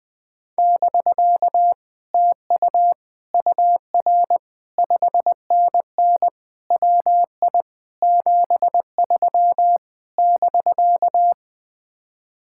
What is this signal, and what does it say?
Morse code, heard as a keyed tone, about 20 wpm, 700 Hz. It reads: <BK> TU UR 5NN WI 73 <BK>
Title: Speed-Racing Common QSO Phrases - 20wpm